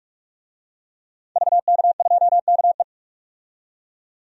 Morse code, heard as a keyed tone, about 46 wpm, 700 Hz. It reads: VK2KE
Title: DXCC-100 Call Signs - 5 Characters - 45wpm